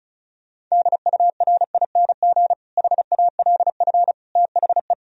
Morse code, heard as a keyed tone, about 34 wpm, 700 Hz. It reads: DURING HALF THE
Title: Sentences from Top 1000 Words - Review Part 2 - 35wpm